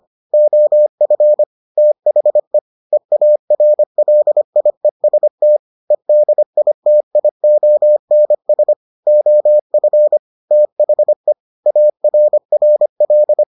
Morse code, heard as a keyed tone, about 25 wpm, 600 Hz. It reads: OF THE EARLIEST EDITIONS OF THE ARRL